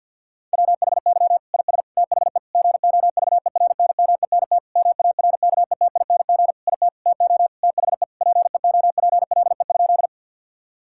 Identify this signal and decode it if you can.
Morse code, heard as a keyed tone, about 50 wpm, 700 Hz. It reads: WHY IS THE GOVERNMENT MARKETING IT TO THE PEOPLE?
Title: Sentences from Top 1000 Words - Review Part 5 - 50wpm